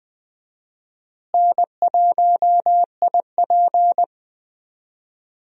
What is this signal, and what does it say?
Morse code, heard as a keyed tone, about 20 wpm, 700 Hz. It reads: N1IP
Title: USA Call Signs - 4 Characters - Repeated 3x - 20wpm